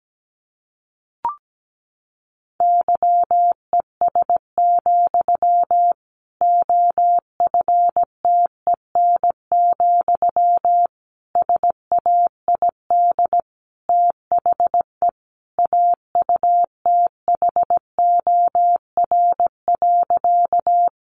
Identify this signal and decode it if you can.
Morse code, heard as a keyed tone, about 17 wpm, 700 Hz. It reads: YES, OFTEN, SAID THE AUTHOR.
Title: Sentences from Top 1200 Words 17wpm